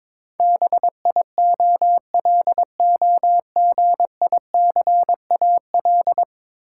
Morse code, heard as a keyed tone, about 22 wpm, 700 Hz. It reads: BIOLOGICAL
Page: morse-code-ninja-practice-22wpm